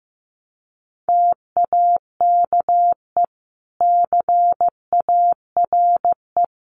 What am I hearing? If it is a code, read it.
Morse code, heard as a keyed tone, about 15 wpm, 700 Hz. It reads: TAKE CARE
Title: Sets of 3 Words 15wpm